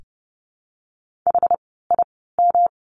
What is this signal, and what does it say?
Morse code, heard as a keyed tone, about 30 wpm, spaced as for 17 wpm, 700 Hz. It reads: HIM
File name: course / 30:10wpm / 108 Lesson 18f - Letter m - Words 30wpm